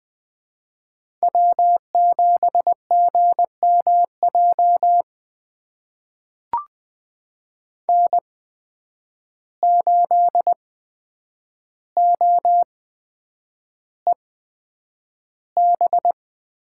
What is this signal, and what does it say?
Morse code, heard as a keyed tone, about 20 wpm, spaced as for 9 wpm, 700 Hz. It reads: W7GMJ N8OEB
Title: USA Call Signs - 5 Characters - 8x Character Spacing - 20wpm